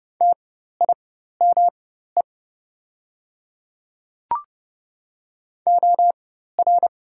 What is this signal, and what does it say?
Morse code, heard as a keyed tone, about 30 wpm, spaced as for 14 wpm, 690 Hz. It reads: TIME OR
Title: 4x Character Spacing - Top 100 Words - 30wpm